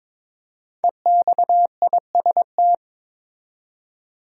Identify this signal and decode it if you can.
Morse code, heard as a keyed tone, about 22 wpm, 700 Hz. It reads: EXIST
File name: 5 Letter Words - First 3 Letters Given - 22wpm